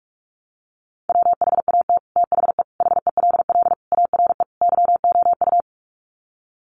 Morse code, heard as a keyed tone, about 44 wpm, 700 Hz. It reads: WHAT THE HELL ARE YOU
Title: Sets of 5 Words Encore 45wpm